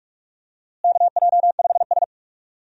Morse code, heard as a keyed tone, about 45 wpm, 700 Hz. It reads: KJ5S
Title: USA Call Signs - 4 Characters - Repeated 2x - 45wpm